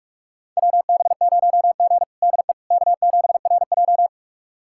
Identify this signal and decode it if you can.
Morse code, heard as a keyed tone, about 46 wpm, 700 Hz. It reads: WB0G DE K7RJ
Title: USA Callsign DE Callsign - 4 Characters - 45wpm